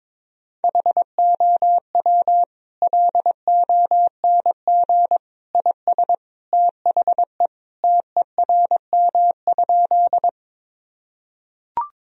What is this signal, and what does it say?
Morse code, heard as a keyed tone, about 21 wpm, 700 Hz. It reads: HOW LONG IS THE TERM?
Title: Sentences from Top 1000 Words - Review Part 4 - 22wpm